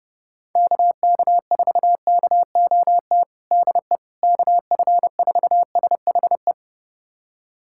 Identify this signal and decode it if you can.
Morse code, heard as a keyed tone, about 30 wpm, 700 Hz. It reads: KK4KOT DE KF4SHE
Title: USA Callsign DE Callsign 30wpm